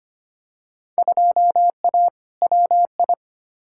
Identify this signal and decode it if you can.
Morse code, heard as a keyed tone, about 25 wpm, 700 Hz. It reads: 2A WI
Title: Field Day Exchanges - 25wpm